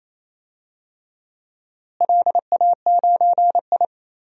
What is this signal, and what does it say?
Morse code, heard as a keyed tone, about 27 wpm, 700 Hz. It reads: LA9I
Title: DXCC-100 Call Signs - 4 Characters - Repeated 3x - 28wpm